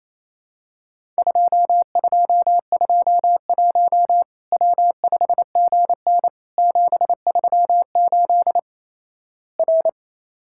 Morse code, heard as a keyed tone, about 27 wpm, 700 Hz. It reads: 2221 W5GN 738 R
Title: Worked All Europe DX Contest - QTC Exchange - 28wpm